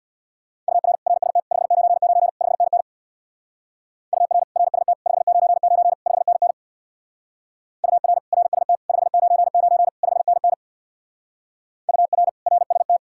Morse code, heard as a keyed tone, about 64 wpm, 700 Hz. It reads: UR RST 599 5NN UR RST 599 5NN UR RST 599 5NN UR RST